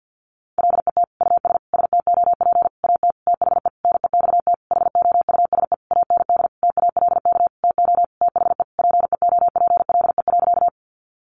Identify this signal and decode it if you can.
Morse code, heard as a keyed tone, about 50 wpm, 700 Hz. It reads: LET US STOP AT THE NEXT HOUSE AND TALK TO THE PEOPLE.
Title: Sentences from Top 400 Words 50wpm